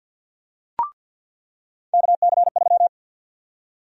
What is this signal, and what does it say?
Morse code, heard as a keyed tone, about 46 wpm, 700 Hz. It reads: KC3
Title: USA Call Sign Prefixes - 50wpm